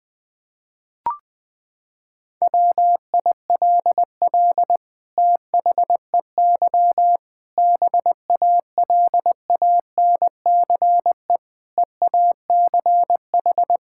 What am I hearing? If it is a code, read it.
Morse code, heard as a keyed tone, about 21 wpm, 700 Hz. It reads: WILL THEY BALANCE EACH
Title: Sentences from Top 1000 Words - Review Part 6 - 20wpm